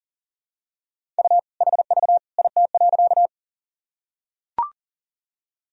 Morse code, heard as a keyed tone, about 40 wpm, 700 Hz. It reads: U HV IT.
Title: Sentences from Top 100 Words and Abbreviations 40wpm